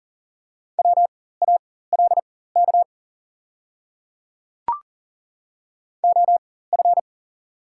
Morse code, heard as a keyed tone, about 40 wpm, spaced as for 19 wpm, 700 Hz. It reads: WALK OF
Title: 4x Character Spacing - Top 500 Words - 40wpm